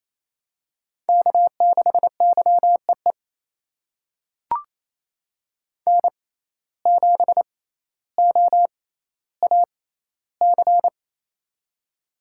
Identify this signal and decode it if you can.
Morse code, heard as a keyed tone, about 28 wpm, spaced as for 14 wpm, 700 Hz. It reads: K6YEE N7OAC
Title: USA Call Signs - 5 Characters - 6x Character Spacing - 28wpm